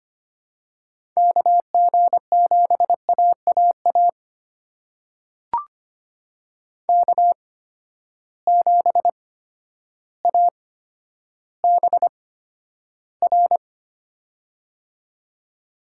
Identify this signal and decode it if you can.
Morse code, heard as a keyed tone, about 25 wpm, spaced as for 11 wpm, 700 Hz. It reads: KG7AAA K7ABR
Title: USA Call Signs - 8x Character Spacing - 25wpm